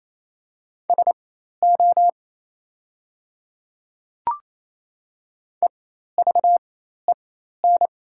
Morse code, heard as a keyed tone, about 27 wpm, spaced as for 13 wpm, 700 Hz. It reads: SO EVEN
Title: 4x Character Spacing - Top 200 Words - 28wpm